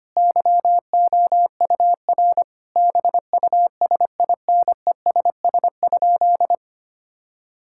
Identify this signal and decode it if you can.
Morse code, heard as a keyed tone, about 25 wpm, 700 Hz. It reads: YOUR BUSINESS?
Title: Sentences from Top 1000 Words - Review Part 6 - 25wpm